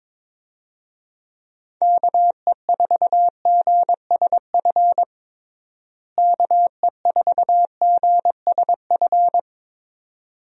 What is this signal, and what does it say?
Morse code, heard as a keyed tone, about 22 wpm, 700 Hz. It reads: KE4GSF KE4GSF
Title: USA Call Signs - 6 Characters - Repeated 2x - 22wpm